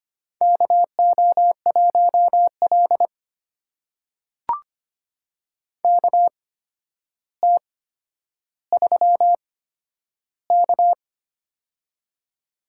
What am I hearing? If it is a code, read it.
Morse code, heard as a keyed tone, about 25 wpm, spaced as for 11 wpm, 700 Hz. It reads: KO1L KT3K
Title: USA Call Signs - 4 Characters - 8x Character Spacing - 25wpm